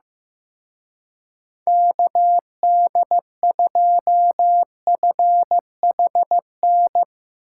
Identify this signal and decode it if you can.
Morse code, heard as a keyed tone, about 15 wpm, 700 Hz. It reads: KD2FHN
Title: Speed Racing - USA Call Signs 15wpm